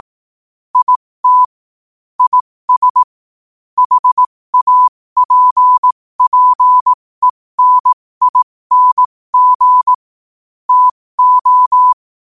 Morse code, heard as a keyed tone, about 18 wpm, spaced as for 15 wpm, 1000 Hz. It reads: IT IS HAPPENING TO
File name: qotdfiles / 15wpm